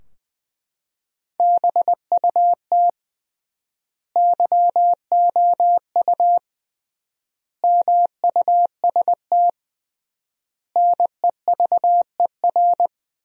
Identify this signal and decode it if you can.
Morse code, heard as a keyed tone, about 21 wpm, 700 Hz. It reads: BUT YOU MUST NEVER
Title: 3x Word-Spacing - Sentences from Top 300 Words - 20wpm